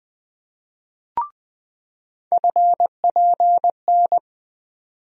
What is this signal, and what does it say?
Morse code, heard as a keyed tone, about 20 wpm, 700 Hz. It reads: FPN
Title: USA Call Sign Postfixes - 3 Letters 20wpm